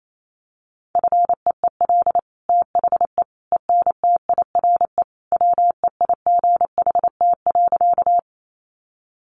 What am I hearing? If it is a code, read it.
Morse code, heard as a keyed tone, about 28 wpm, 700 Hz. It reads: FEEL THE ENTIRE WEIGHT.
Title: Sentences from Top 800 Words 28wpm